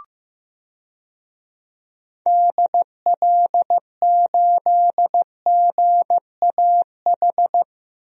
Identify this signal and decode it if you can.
Morse code, heard as a keyed tone, about 15 wpm, 700 Hz. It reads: DL8GAH
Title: DXCC-100 Call Signs - 6 Characters - 15wpm